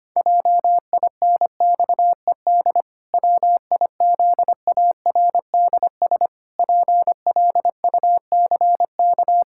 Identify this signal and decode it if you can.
Morse code, heard as a keyed tone, about 25 wpm, 700 Hz. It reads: JINXED WIZARDS PLUCK
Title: Pangram Challenge v2 - 25wpm